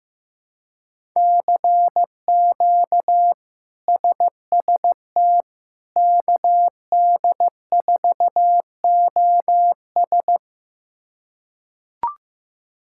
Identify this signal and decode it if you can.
Morse code, heard as a keyed tone, about 15 wpm, 700 Hz. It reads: CQ SST KD4OS
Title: K1USN SST QSO Practice - 15wpm